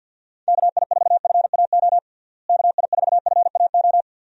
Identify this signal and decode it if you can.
Morse code, heard as a keyed tone, about 47 wpm, 700 Hz. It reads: KI4FAO KI4FAO
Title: USA Call Signs - 6 Characters - Repeated 3x - 50wpm